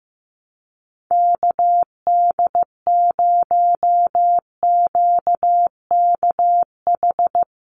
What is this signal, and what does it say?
Morse code, heard as a keyed tone, about 15 wpm, 700 Hz. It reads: KD0QKH
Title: USA Call Signs 15wpm